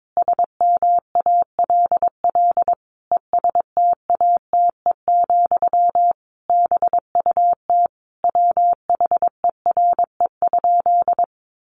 Morse code, heard as a keyed tone, about 22 wpm, 700 Hz. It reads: SMALL ESTATE, BUT WHERE?
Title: Sentences from Top 1200 Words 22wpm